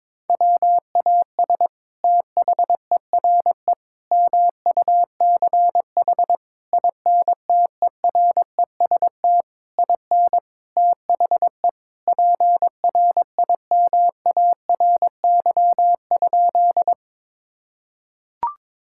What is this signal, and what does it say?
Morse code, heard as a keyed tone, about 23 wpm, 700 Hz. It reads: WAS THERE MUCH INTEREST IN THE PRIMARY?
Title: Sentences from Top 1000 Words - Review Part 5 - 22wpm